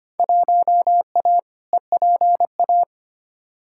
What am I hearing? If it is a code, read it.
Morse code, heard as a keyed tone, about 25 wpm, 700 Hz. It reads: 1A EPA
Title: Field Day Exchanges - 25wpm